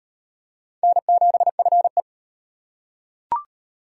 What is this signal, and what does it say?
Morse code, heard as a keyed tone, about 34 wpm, 700 Hz. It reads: N7FE